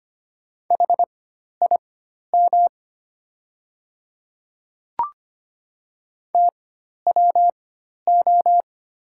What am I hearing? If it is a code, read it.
Morse code, heard as a keyed tone, about 25 wpm, spaced as for 12 wpm, 690 Hz. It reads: HIM TWO
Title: 4x Character Spacing - Top 100 Words - 25wpm